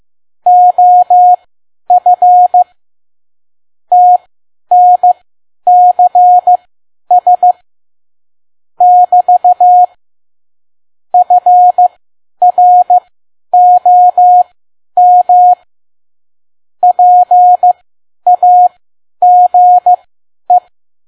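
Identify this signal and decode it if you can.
Morse code, heard as a keyed tone, about 15 wpm, spaced as for 10 wpm, 720 Hz. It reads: OF TNCS <BT> FROM PAGE
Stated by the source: Morse code practice at 10 WPM.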